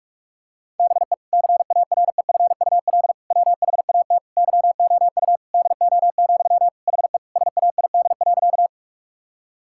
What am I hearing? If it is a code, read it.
Morse code, heard as a keyed tone, about 46 wpm, 700 Hz. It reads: BE CAREFUL WHAT YOU DO, HE SAID.
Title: Sentences from Top 1600 Words 45wpm